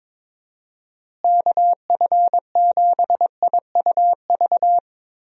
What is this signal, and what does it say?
Morse code, heard as a keyed tone, about 22 wpm, 700 Hz. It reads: KF7IUV
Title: USA Call Signs - 6 Characters - Repeated 2x - 22wpm